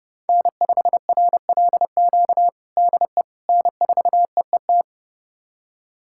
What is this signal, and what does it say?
Morse code, heard as a keyed tone, about 31 wpm, 700 Hz. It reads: N5RLQ DE N4EET
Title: USA Callsign DE Callsign - 5 Characters - 30wpm